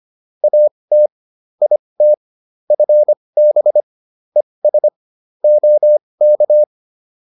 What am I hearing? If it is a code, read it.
Morse code, heard as a keyed tone, about 25 wpm, spaced as for 20 wpm, 600 Hz. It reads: AT IT FB ES OK